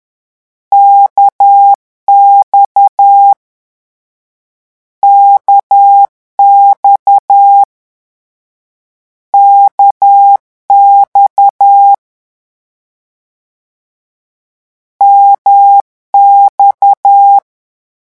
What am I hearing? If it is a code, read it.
Morse code, heard as a keyed tone, about 10 wpm, 800 Hz. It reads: KX KX KX MX
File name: morse mp3 / lecon1-vitesse 10